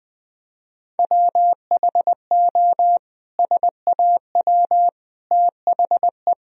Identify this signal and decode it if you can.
Morse code, heard as a keyed tone, about 20 wpm, 700 Hz. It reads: WHO SAW THE
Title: Sentences from Top 1000 Words - Review Part 1 - 20wpm